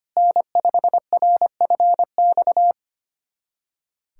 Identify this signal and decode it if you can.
Morse code, heard as a keyed tone, about 25 wpm, 700 Hz. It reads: N5RFX
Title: USA Callsign DE Callsign 25wpm